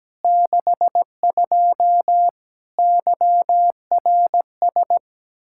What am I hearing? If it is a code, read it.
Morse code, heard as a keyed tone, about 17 wpm, 700 Hz. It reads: 62 YRS
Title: K1USN SST QSO Practice 17wpm